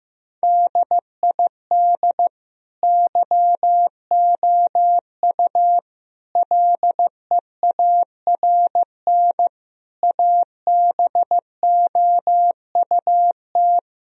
Morse code, heard as a keyed tone, about 15 wpm, 700 Hz. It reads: DID YOU LEARN ABOUT
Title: Sentences from Top 1000 Words - Review Part 4 - 15wpm